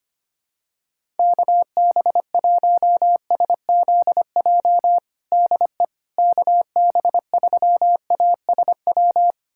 Morse code, heard as a keyed tone, about 25 wpm, 700 Hz. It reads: KB1SZJ DE KB3ASW
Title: USA Callsign DE Callsign - 6 Characters - 25wpm